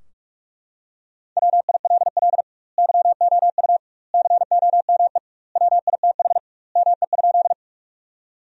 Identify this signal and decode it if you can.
Morse code, heard as a keyed tone, about 43 wpm, 700 Hz. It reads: WILL YOU COME WITH ME?
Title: 2x Word-Spacing - Sentences from Top 200 Words - 45wpm